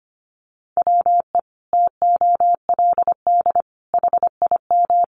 Morse code, heard as a keyed tone, about 25 wpm, 700 Hz. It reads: WE TOLD HIM
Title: Sentences from Top 1300 Words 25wpm